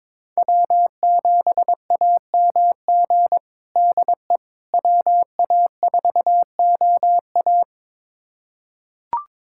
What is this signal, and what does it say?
Morse code, heard as a keyed tone, about 22 wpm, 700 Hz. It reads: W7AMG DE WA4OA
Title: USA Callsign DE Callsign - 5 Characters - 22wpm